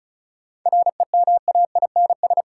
Morse code, heard as a keyed tone, about 36 wpm, 700 Hz. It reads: REMAINS
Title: Sets of 5 Words Encore - New Word Drill - 35wpm